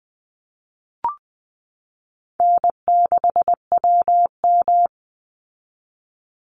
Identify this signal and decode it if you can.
Morse code, heard as a keyed tone, about 20 wpm, 700 Hz. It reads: N6WM
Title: Easy Call Signs Repeated 2x - 20wpm